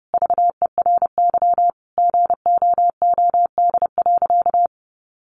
Morse code, heard as a keyed tone, about 30 wpm, 700 Hz. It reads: VERY GOOD.
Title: Sentences from Top 800 Words 30wpm